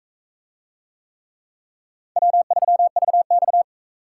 Morse code, heard as a keyed tone, about 37 wpm, 700 Hz. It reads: W3VX